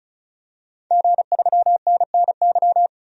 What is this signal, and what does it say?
Morse code, heard as a keyed tone, about 34 wpm, 700 Hz. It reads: G3NNY
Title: DXCC-100 Call Signs - 5 Characters - 35wpm